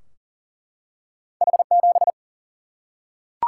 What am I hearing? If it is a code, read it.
Morse code, heard as a keyed tone, about 40 wpm, 700 Hz. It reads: H7
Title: Single-Letter Single-Number 40wpm